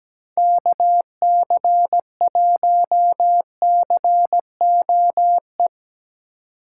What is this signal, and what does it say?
Morse code, heard as a keyed tone, about 18 wpm, 700 Hz. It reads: KC1COE
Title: Speed Racing - USA Call Signs 17wpm